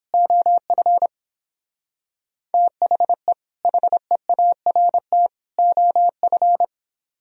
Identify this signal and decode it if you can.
Morse code, heard as a keyed tone, about 26 wpm, 700 Hz. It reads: OF THE HEART OF
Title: Speed-Racing - Sets of 3 Words 22wpm